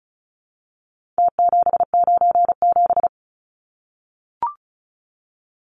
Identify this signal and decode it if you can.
Morse code, heard as a keyed tone, about 34 wpm, 700 Hz. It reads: T797
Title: Numbers - Contest Style - 4-Digit 35wpm